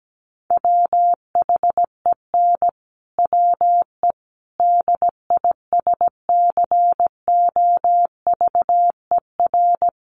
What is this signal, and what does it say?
Morse code, heard as a keyed tone, about 17 wpm, 700 Hz. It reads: WHEN WE DISCOVER
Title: Sentences from Top 1100 Words 17wpm